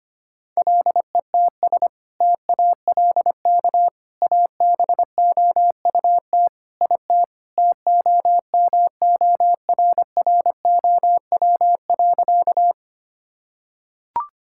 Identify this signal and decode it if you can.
Morse code, heard as a keyed tone, about 25 wpm, 700 Hz. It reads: LETS TALK ABOUT IT TOMORROW.
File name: practice / 25wpm / Sentences with 2-Word Phrases - 25wpm